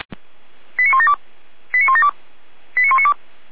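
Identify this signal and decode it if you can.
Five-tone selective call (ZVEI), heard as quick runs of five tones, tones of about 70 ms. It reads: 89172 89172 89192
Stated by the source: ZVEI five-tone call received by SDR.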